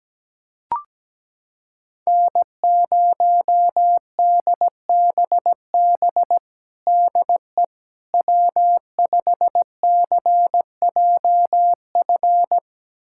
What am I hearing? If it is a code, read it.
Morse code, heard as a keyed tone, about 17 wpm, 700 Hz. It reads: N0DBB DE W5CJF
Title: USA Callsign DE Callsign - 5 Characters - 17wpm